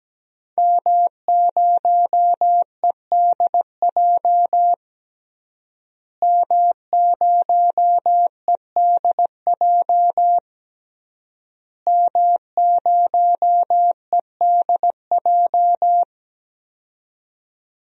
Morse code, heard as a keyed tone, about 17 wpm, 700 Hz. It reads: M0EDJ M0EDJ M0EDJ
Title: DXCC-100 Call Signs - 5 Characters - Repeated 3x - 17wpm